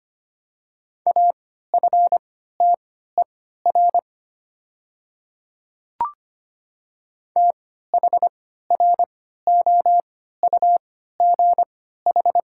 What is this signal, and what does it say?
Morse code, heard as a keyed tone, about 25 wpm, spaced as for 14 wpm, 700 Hz. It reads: AFTER THROUGH
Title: 3x Character Spacing - Top 200 Words - 25wpm